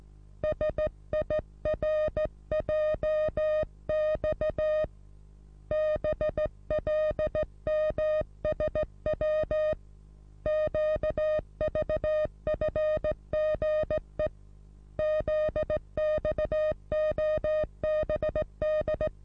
Morse code, heard as a keyed tone, about 17 wpm, 620 Hz. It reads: SIRJX BLMSW QVFGE ZXOBD
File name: repos / Morsekurs / DJ4UF Morsen27